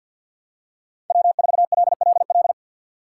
Morse code, heard as a keyed tone, about 53 wpm, 700 Hz. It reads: W4LLL
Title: USA Call Signs - 5 Characters - 3x Character Spacing - 50wpm